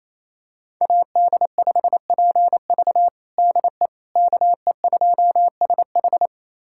Morse code, heard as a keyed tone, about 28 wpm, 700 Hz. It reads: AD5PV DE KE2SH
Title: USA Callsign DE Callsign - 5 Characters - 28wpm